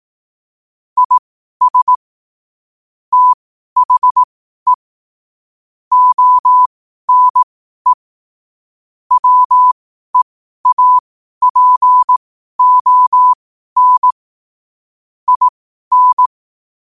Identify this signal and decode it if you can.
Morse code, heard as a keyed tone, about 18 wpm, spaced as for 12 wpm, 1000 Hz. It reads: IS THE ONE WEAPON IN
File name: qotdfiles / 12wpm